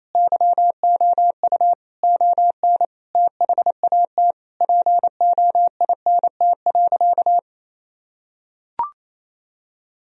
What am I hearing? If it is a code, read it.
Morse code, heard as a keyed tone, about 28 wpm, 700 Hz. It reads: YOU ON THAT POINT.
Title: Sentences from Top 1000 Words - Review Part 1 - 28wpm